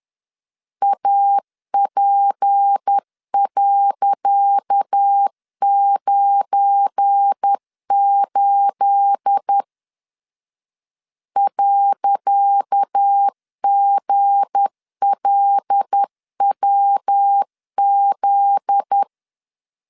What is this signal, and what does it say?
Morse code, heard as a keyed tone, about 10 wpm, 800 Hz. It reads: AP.98 .GLWZ
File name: LEZIONI / 10 SIGNS / 10C DOT 10wpm 2940